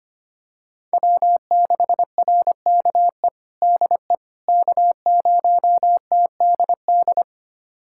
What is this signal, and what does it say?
Morse code, heard as a keyed tone, about 25 wpm, 700 Hz. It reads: W6RKE DE K0TDD
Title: USA Callsign DE Callsign - 5 Characters - 25wpm